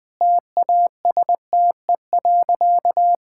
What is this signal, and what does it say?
Morse code, heard as a keyed tone, about 20 wpm, 700 Hz. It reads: TASTE.
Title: Sentences with 5-Word Phrases - 20wpm